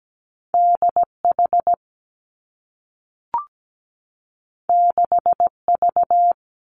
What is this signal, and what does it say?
Morse code, heard as a keyed tone, about 17 wpm, 700 Hz. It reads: D H 6 V
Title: Two Challenging Letter-Number Combinations 17wpm